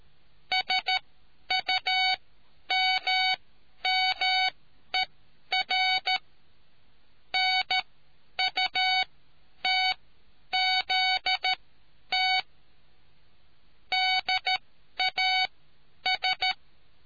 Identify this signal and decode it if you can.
Morse code, heard as a keyed tone, about 13 wpm, spaced as for 10 wpm, 750 Hz. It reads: SUMMER NUTZT DAS